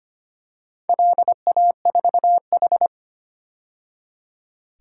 Morse code, heard as a keyed tone, about 25 wpm, 700 Hz. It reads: LA4H
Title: DXCC-100 Call Signs - 4 Characters - 25wpm